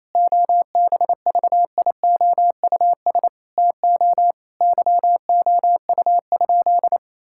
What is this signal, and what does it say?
Morse code, heard as a keyed tone, about 28 wpm, 700 Hz. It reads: OBVIOUS TO YOU?
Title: Sentences from Top 1600 Words 28wpm